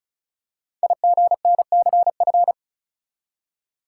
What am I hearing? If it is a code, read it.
Morse code, heard as a keyed tone, about 36 wpm, 690 Hz. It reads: IGNCF
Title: Code Groups - 5 Letters - 35wpm